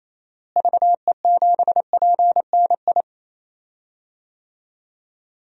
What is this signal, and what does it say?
Morse code, heard as a keyed tone, about 28 wpm, 700 Hz. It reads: VE7PNI
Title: DXCC-100 Call Signs - 6 Characters - Repeated 3x - 28wpm